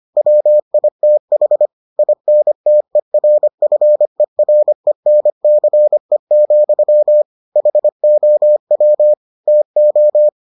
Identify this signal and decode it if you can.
Morse code, heard as a keyed tone, about 25 wpm, 600 Hz. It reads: WITH INTERFERENCE, HOW TO